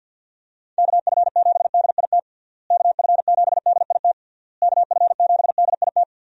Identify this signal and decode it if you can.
Morse code, heard as a keyed tone, about 49 wpm, 700 Hz. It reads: KF7DIT KF7DIT KF7DIT
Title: USA Call Signs - 6 Characters - Repeated 3x - 50wpm